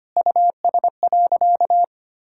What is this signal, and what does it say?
Morse code, heard as a keyed tone, about 25 wpm, 700 Hz. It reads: US.
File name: practice / 25wpm / Sentences with 4-Word Phrases - 25wpm